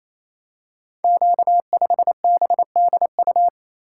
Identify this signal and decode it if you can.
Morse code, heard as a keyed tone, about 29 wpm, 700 Hz. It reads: Q5BDU
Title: Code Groups - 5 Letter:Numbers - 28wpm